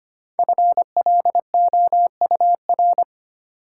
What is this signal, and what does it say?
Morse code, heard as a keyed tone, about 25 wpm, 700 Hz. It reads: FLOUR
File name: 5 Letter Words - First 3 Letters Given - 25wpm